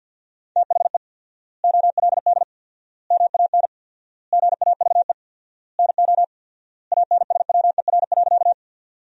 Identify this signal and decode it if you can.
Morse code, heard as a keyed tone, about 52 wpm, 700 Hz. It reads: THE OLD MAN GAVE NO ANSWER.
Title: 4x Word-Spacing - Sentences from Top 500 Words - 50wpm